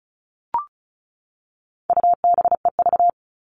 Morse code, heard as a keyed tone, about 34 wpm, 700 Hz. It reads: UBEV
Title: All Cut Numbers - 4-Digit 35wpm